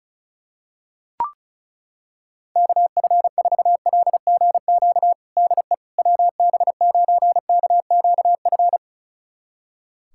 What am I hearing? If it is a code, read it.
Morse code, heard as a keyed tone, about 34 wpm, 700 Hz. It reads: KF4LGQ DE WB9KQF
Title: USA Callsign DE Callsign - 6 Characters - 35wpm